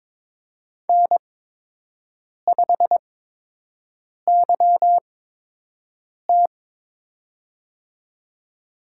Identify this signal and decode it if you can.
Morse code, heard as a keyed tone, about 22 wpm, spaced as for 6 wpm, 700 Hz. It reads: N5YT
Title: USA Call Signs - 4 Characters - 8x Character Spacing - 22wpm